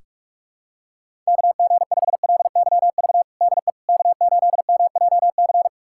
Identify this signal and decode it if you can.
Morse code, heard as a keyed tone, about 46 wpm, 700 Hz. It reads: KG5LYV DE K8MJC
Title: USA Callsign DE Callsign 45wpm